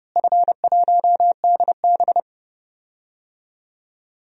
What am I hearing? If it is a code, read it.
Morse code, heard as a keyed tone, about 30 wpm, 700 Hz. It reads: F1DB
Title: DXCC-100 Call Signs - 4 Characters - Repeated 2x - 30wpm